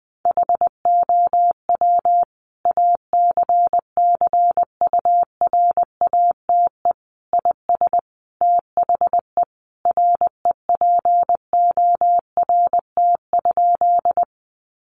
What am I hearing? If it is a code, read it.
Morse code, heard as a keyed tone, about 19 wpm, 700 Hz. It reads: HOW ACCURATE IS THE REPORT?
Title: Sentences from Top 1200 Words 20wpm